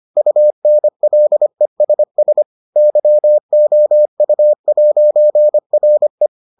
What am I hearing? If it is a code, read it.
Morse code, heard as a keyed tone, about 25 wpm, 600 Hz. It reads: UNLESS YOU'RE